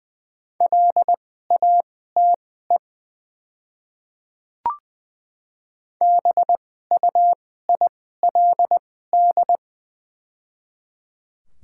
Morse code, heard as a keyed tone, about 20 wpm, spaced as for 14 wpm, 700 Hz. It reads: LATE BUILD
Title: 2x Character Spacing - Top 500 Words - 20wpm